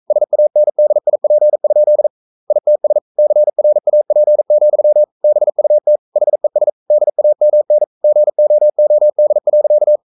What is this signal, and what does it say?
Morse code, heard as a keyed tone, about 42 wpm, 600 Hz. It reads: SANDIP? ITS CRAP, BUT HES DAMN GOOD.
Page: Romanen Never af Ken Follet på engelsk.